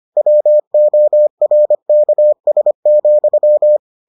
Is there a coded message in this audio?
Yes: Morse code from 0.2 to 3.8 s, heard as a keyed tone, about 25 wpm, 600 Hz. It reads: WORKS,